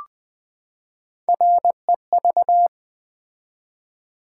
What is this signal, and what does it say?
Morse code, heard as a keyed tone, about 20 wpm, 700 Hz. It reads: REV
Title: Build A Word - 3 Letters At A Time - 6-Letter Words - 20wpm